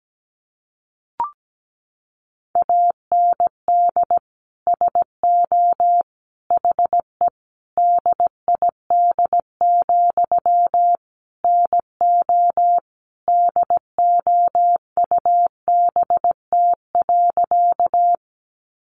Morse code, heard as a keyed tone, about 17 wpm, 700 Hz. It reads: AND SO HE DID, NO DOUBT.
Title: Sentences from Top 1500 Words 17wpm